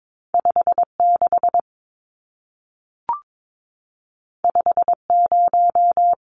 Morse code, heard as a keyed tone, about 22 wpm, 700 Hz. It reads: 56 50
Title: Numbers - 2-Digit 22wpm